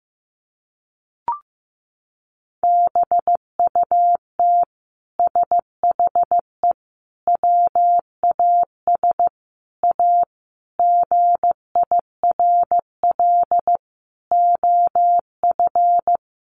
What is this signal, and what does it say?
Morse code, heard as a keyed tone, about 15 wpm, 700 Hz. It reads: BUT SHE WAS A GIRL OF
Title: Sentences from Top 1400 Words 15wpm